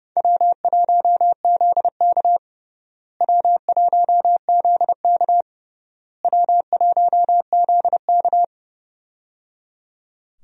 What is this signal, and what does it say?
Morse code, heard as a keyed tone, about 30 wpm, 700 Hz. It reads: W1ZK W1ZK W1ZK
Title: USA Call Signs - 4 Characters - Repeated 3x - 30wpm